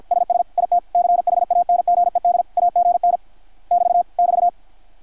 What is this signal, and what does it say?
Morse code, heard as a keyed tone, about 53 wpm, 720 Hz. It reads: SS IT CHANGED AGN <BT> <BT>
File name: archived / CRSnet